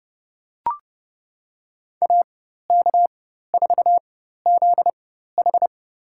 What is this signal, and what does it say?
Morse code, heard as a keyed tone, about 31 wpm, spaced as for 14 wpm, 700 Hz. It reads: AK4ZH
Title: USA Call Signs - 5 Characters - 4x Character Spacing - 30wpm